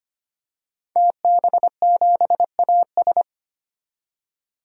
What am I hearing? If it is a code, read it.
Morse code, heard as a keyed tone, about 25 wpm, 700 Hz. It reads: TB7AS
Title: DXCC-100 Call Signs - 5 Characters - 25wpm